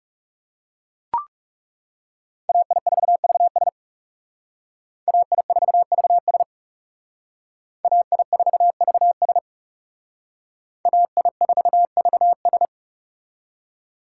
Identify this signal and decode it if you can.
Morse code, heard as a keyed tone, about 36 wpm, 700 Hz. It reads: AI4VS AI4VS AI4VS AI4VS
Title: Speed Racing - USA 2x2 Call Signs 30wpm